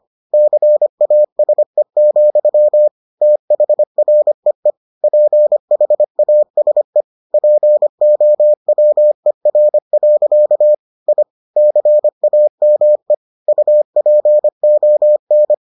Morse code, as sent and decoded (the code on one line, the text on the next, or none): -.-. .- ... --..-- / - .... .-. . / .--. .... .- ... / .--. --- .-- . .-. .-.-.- / .. / -.-. .- -- . / ..- .--. --- -.
CASE, THREE PHASE POWER. I CAME UPON